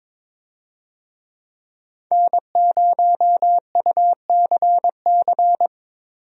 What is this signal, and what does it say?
Morse code, heard as a keyed tone, about 22 wpm, 700 Hz. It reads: N0UCC
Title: USA Call Signs - 5 Characters - Repeated 2x - 22wpm